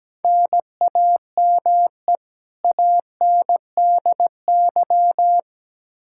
Morse code, heard as a keyed tone, about 17 wpm, 700 Hz. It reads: NAME ANDY
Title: Practice QSO - Exchange 1 - Beginning - 17wpm